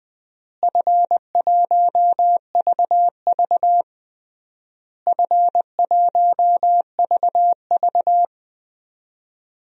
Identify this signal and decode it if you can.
Morse code, heard as a keyed tone, about 21 wpm, 700 Hz. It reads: F1VV F1VV
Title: DXCC-100 Call Signs - 4 Characters - Repeated 3x - 20wpm